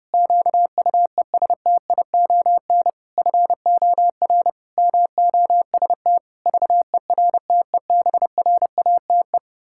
Morse code, heard as a keyed tone, about 30 wpm, 700 Hz. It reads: QUESTION FOR MOST VERTEBRATE
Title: Pangram Challenge v2 - 30wpm